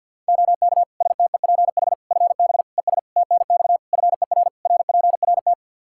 Morse code, heard as a keyed tone, about 52 wpm, 700 Hz. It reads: OK STEPH FB ES TNX FER RPRT
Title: Practice QSO - Exchange 2 - Info - 50wpm